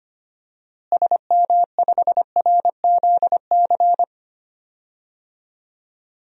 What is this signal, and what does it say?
Morse code, heard as a keyed tone, about 25 wpm, 700 Hz. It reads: SM5RZC
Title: DXCC-100 Call Signs - 6 Characters - Repeated 2x - 25wpm